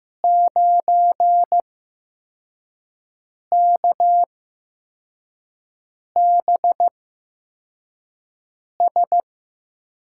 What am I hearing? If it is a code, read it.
Morse code, heard as a keyed tone, about 15 wpm, spaced as for 4 wpm, 700 Hz. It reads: 9KBS
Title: USA Call Signs - 5 Characters - 8x Character Spacing - 15wpm